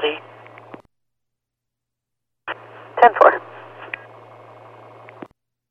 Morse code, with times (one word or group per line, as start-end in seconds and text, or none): none